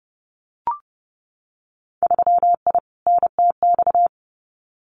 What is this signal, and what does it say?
Morse code, heard as a keyed tone, about 30 wpm, 700 Hz. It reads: 3I NTX
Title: Winter Field Day Exchanges 30wpm